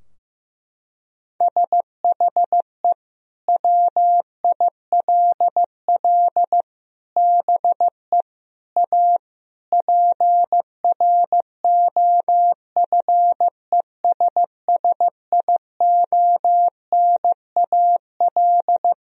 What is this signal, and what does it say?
Morse code, heard as a keyed tone, about 15 wpm, 700 Hz. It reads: SHE WILL BE A PROFESSIONAL
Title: Sentences from Top 1000 Words - Review Part 4 - 15wpm